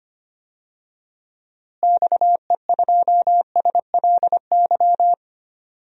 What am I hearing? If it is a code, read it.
Morse code, heard as a keyed tone, about 25 wpm, 700 Hz. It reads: XE2SLY
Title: DXCC-100 Call Signs - 6 Characters - Repeated 3x - 25wpm